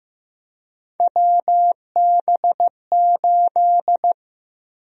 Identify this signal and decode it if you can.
Morse code, heard as a keyed tone, about 15 wpm, 700 Hz. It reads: WB8
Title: USA Call Sign Prefixes - 15wpm